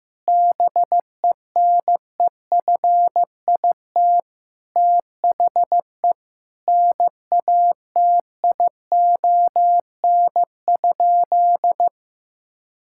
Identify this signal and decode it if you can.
Morse code, heard as a keyed tone, about 14 wpm, 700 Hz. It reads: BENEFIT THE NATION?
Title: Sentences from Top 1000 Words - Review Part 6 - 15wpm